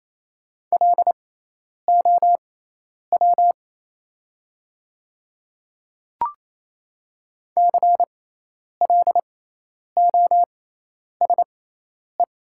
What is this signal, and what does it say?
Morse code, heard as a keyed tone, about 28 wpm, spaced as for 10 wpm, 700 Hz. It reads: LOW CLOSE
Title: 6x Character Spacing - Top 300 Words - 28wpm